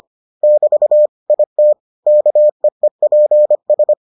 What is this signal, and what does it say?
Morse code, heard as a keyed tone, about 25 wpm, 600 Hz. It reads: <BT> IT KEEPS